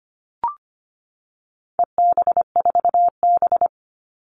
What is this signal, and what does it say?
Morse code, heard as a keyed tone, about 25 wpm, 700 Hz. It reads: EB4B
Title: All Cut Numbers - 4-Digit 25wpm